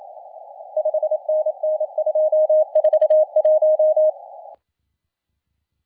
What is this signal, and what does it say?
Morse code, heard as a keyed tone, about 27 wpm, 620 Hz. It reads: 5NN 241
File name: la4c / sac-cw-2017 / CONTEST AUDIO 1100Z